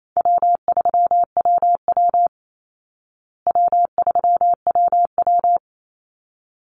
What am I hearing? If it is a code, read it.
Morse code, heard as a keyed tone, about 28 wpm, 700 Hz. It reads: W3WW W3WW
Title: Easy Call Signs Repeated 2x - 28wpm